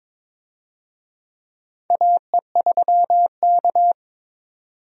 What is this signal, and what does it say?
Morse code, heard as a keyed tone, about 22 wpm, 700 Hz. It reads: AE3K
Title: USA Call Signs - 4 Characters - 22wpm